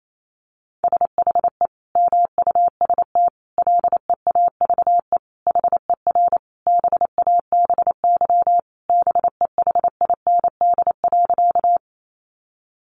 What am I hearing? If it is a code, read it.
Morse code, heard as a keyed tone, about 28 wpm, 700 Hz. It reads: SHE MUST LEAVE HER BABY BEHIND.
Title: Sentences from Top 1300 Words 28wpm